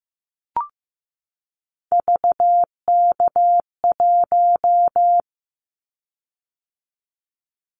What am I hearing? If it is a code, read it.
Morse code, heard as a keyed tone, about 15 wpm, 700 Hz. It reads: VK1
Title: Callsign Prefixes for Top 100 DXCC - 15wpm